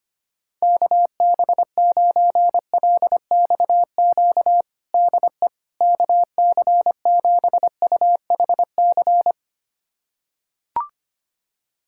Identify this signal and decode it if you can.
Morse code, heard as a keyed tone, about 25 wpm, 700 Hz. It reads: KB9LXQ DE KC7UHC
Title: USA Callsign DE Callsign - 6 Characters - 25wpm